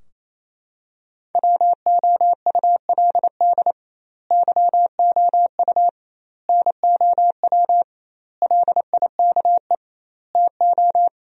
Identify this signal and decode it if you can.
Morse code, heard as a keyed tone, about 28 wpm, 700 Hz. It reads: WOULD YOU NOW LIKE TO
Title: 2x Word-Spacing - Sentences from Top 200 Words - 28wpm